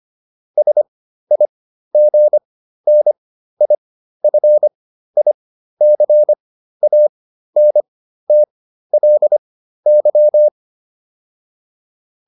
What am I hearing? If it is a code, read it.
Morse code, heard as a keyed tone, about 25 wpm, spaced as for 13 wpm, 600 Hz. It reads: SIGNIFICANTLY